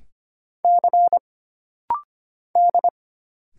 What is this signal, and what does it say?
Morse code, heard as a keyed tone, about 25 wpm, 700 Hz. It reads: C D
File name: Single Character (Letter:Number:Punctuation:Prosign) - Rapid-Fire - 25wpm